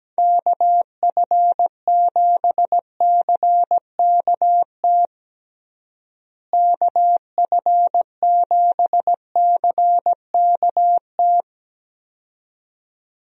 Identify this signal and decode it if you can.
Morse code, heard as a keyed tone, about 17 wpm, 700 Hz. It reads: KF7CKT KF7CKT
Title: USA Call Signs - 6 Characters - Repeated 2x - 17wpm